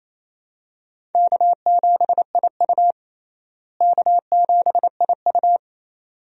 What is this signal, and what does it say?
Morse code, heard as a keyed tone, about 28 wpm, 700 Hz. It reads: K7IU K7IU
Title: USA Call Signs - 4 Characters - Repeated 2x - 28wpm